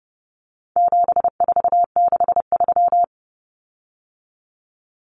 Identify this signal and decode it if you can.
Morse code, heard as a keyed tone, about 30 wpm, 700 Hz. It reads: 7463
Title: Common Cut Numbers - 4-Digit 30wpm